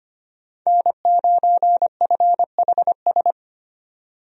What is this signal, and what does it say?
Morse code, heard as a keyed tone, about 25 wpm, 700 Hz. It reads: N9FHS
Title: USA Call Signs - 5 Characters - 25wpm